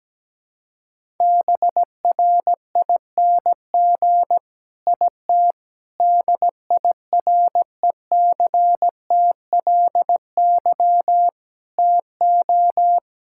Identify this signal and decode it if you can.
Morse code, heard as a keyed tone, about 17 wpm, 700 Hz. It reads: BRING IT DIRECTLY TO
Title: Sentences from Top 1000 Words - Review Part 5 - 17wpm